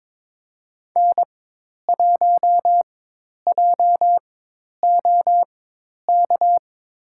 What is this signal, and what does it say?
Morse code, heard as a keyed tone, about 22 wpm, spaced as for 10 wpm, 700 Hz. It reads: N1JOK
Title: USA Call Signs - 5 Characters - 4x Character Spacing - 22wpm